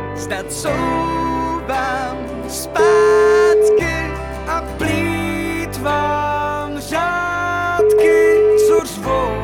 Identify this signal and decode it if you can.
Ringback tone (Europe, 425 Hz), with noise.